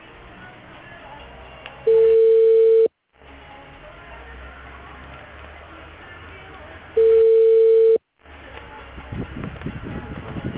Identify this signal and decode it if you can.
Ringback tone (China, 450 Hz), with speech.